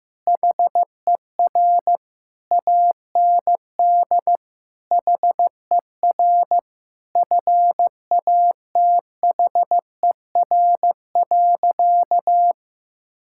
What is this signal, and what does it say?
Morse code, heard as a keyed tone, about 15 wpm, 700 Hz. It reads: HER AND HER FATHER.
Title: Sentences from Top 1000 Words - Review Part 6 - 15wpm